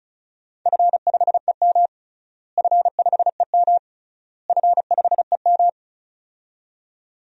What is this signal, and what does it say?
Morse code, heard as a keyed tone, about 36 wpm, 700 Hz. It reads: F5EM F5EM F5EM
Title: DXCC-100 Call Signs - 4 Characters - Repeated 3x - 35wpm